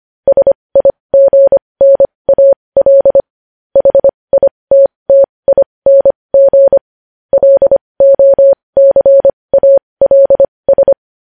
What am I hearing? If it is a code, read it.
Morse code, heard as a keyed tone, about 25 wpm, spaced as for 20 wpm, 550 Hz. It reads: SIGNAL HITTING LOCALS
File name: podcast / News headlines-1x-25x20